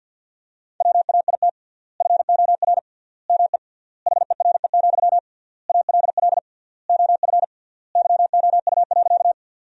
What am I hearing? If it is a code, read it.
Morse code, heard as a keyed tone, about 50 wpm, 700 Hz. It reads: WAIT FOR ME HERE, ALL OF YOU.
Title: 3x Word-Spacing - Sentences from Top 500 Words - 50wpm